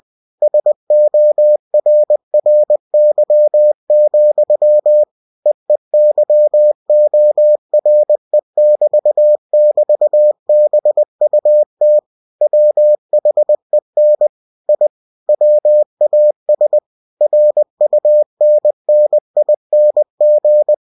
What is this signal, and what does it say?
Morse code, heard as a keyed tone, about 20 wpm, 600 Hz. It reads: SORRY, EEYORE<BT><BT>BUT WHEN I WAS RUNNING
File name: cw-books / winnie-the-pooh / winnie-20wpm-page-0055